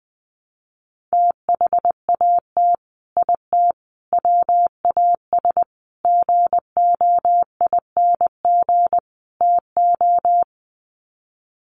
Morse code, heard as a keyed tone, about 20 wpm, 700 Hz. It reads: THAT IT WAS GOING TO
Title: Sets of 5 Words Encore2 20wpm